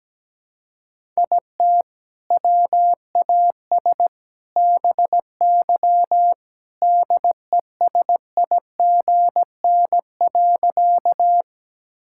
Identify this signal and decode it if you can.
Morse code, heard as a keyed tone, about 17 wpm, 700 Hz. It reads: IT WAS BY DESIGN.
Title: Sentences from Top 1000 Words - Review Part 5 - 17wpm